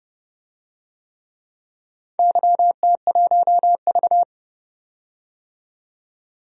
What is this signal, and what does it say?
Morse code, heard as a keyed tone, about 30 wpm, 700 Hz. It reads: YT1V
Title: DXCC-100 Call Signs - 4 Characters - 30wpm